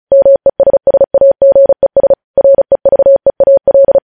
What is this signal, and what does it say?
Morse code, heard as a keyed tone, about 36 wpm, 550 Hz. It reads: MESSAGES REVEAL
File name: podcast / News headlines-1x-35x35